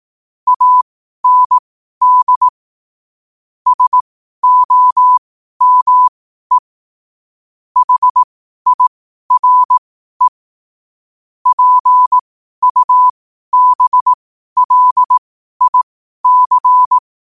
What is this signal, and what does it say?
Morse code, heard as a keyed tone, about 18 wpm, spaced as for 12 wpm, 1000 Hz. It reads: AND SOME HIRE PUBLIC
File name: qotdfiles / 12wpm